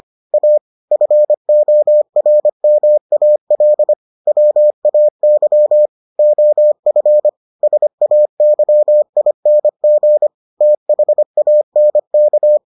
Morse code, heard as a keyed tone, about 25 wpm, 600 Hz. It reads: A FORMAL WAY OF SAYING THANK